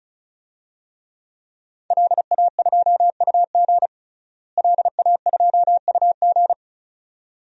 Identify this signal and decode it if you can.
Morse code, heard as a keyed tone, about 36 wpm, 700 Hz. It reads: LA2UG LA2UG
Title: DXCC-100 Call Signs - 5 Characters - Repeated 2x - 35wpm